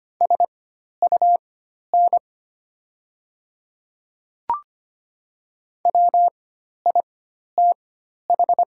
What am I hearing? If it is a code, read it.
Morse code, heard as a keyed tone, about 25 wpm, spaced as for 12 wpm, 700 Hz. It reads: SUN WITH
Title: 4x Character Spacing - Top 300 Words - 25wpm